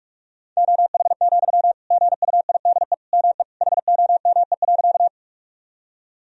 Morse code, heard as a keyed tone, about 46 wpm, 700 Hz. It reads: OH, GUIDE ME HOME.
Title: Sentences from Top 1000 Words - Review Part 3 - 45wpm